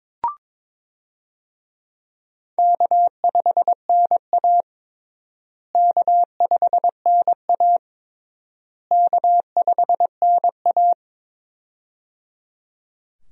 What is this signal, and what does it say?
Morse code, heard as a keyed tone, about 22 wpm, 700 Hz. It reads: K5NA K5NA K5NA
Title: USA Call Signs - 4 Characters - Repeated 3x - 22wpm